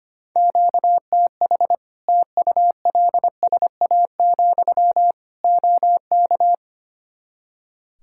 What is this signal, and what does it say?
Morse code, heard as a keyed tone, about 25 wpm, 700 Hz. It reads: QTH TULSA, OK
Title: Practice QSO - Exchange 1 - Beginning - 25wpm